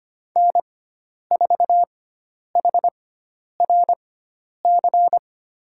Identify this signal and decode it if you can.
Morse code, heard as a keyed tone, about 25 wpm, spaced as for 10 wpm, 700 Hz. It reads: N4HRC
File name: course / 25:10wpm / 095 Lesson 16g - Letter c - Call Signs 25wpm